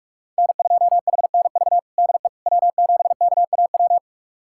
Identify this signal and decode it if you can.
Morse code, heard as a keyed tone, about 46 wpm, 700 Hz. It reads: N2HNV DE W7KAW
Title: USA Callsign DE Callsign - 5 Characters - 45wpm